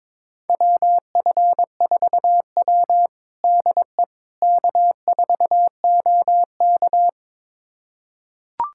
Morse code, heard as a keyed tone, about 22 wpm, 700 Hz. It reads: WF4W DE K4OK
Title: USA Callsign DE Callsign - 4 Characters - 22wpm